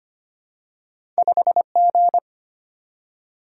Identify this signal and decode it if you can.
Morse code, heard as a keyed tone, about 25 wpm, 700 Hz. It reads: 5 G
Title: Single-Letter Single-Number 25wpm